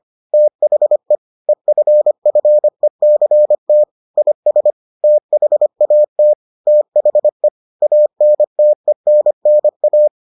Morse code, heard as a keyed tone, about 25 wpm, 600 Hz. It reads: THE EFFECT IS THAT THE ANTENNA